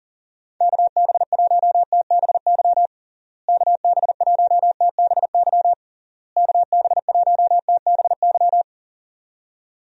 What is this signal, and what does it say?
Morse code, heard as a keyed tone, about 40 wpm, 700 Hz. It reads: KB1TBY KB1TBY KB1TBY
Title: USA Call Signs - 6 Characters - Repeated 3x - 40wpm